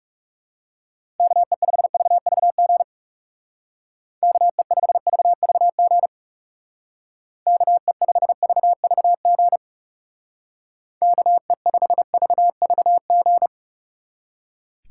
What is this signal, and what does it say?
Morse code, heard as a keyed tone, about 37 wpm, 700 Hz. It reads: KE5VVG KE5VVG KE5VVG KE5VVG
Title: Speed Racing - USA 2x3 Call Signs 30wpm